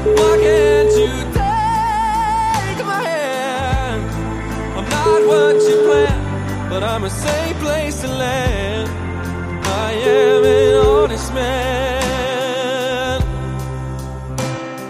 Ringback tone (Europe, 425 Hz), with noise.